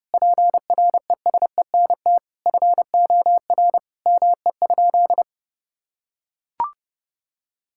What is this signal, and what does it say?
Morse code, heard as a keyed tone, about 32 wpm, 700 Hz. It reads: PRESENT FOR ME?
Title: Sentences from Top 1000 Words - Review Part 5 - 30wpm